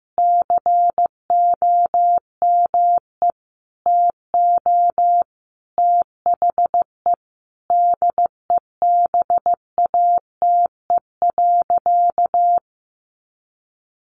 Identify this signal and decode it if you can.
Morse code, heard as a keyed tone, about 15 wpm, 700 Hz. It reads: COME TO THE DEBATE.
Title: Sentences from Top 1100 Words 15wpm